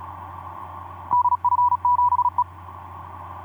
Morse code, heard as a keyed tone, about 37 wpm, 980 Hz. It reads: DF7E
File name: d44ac ek2rgt etc 10m cw 23nov2010 1202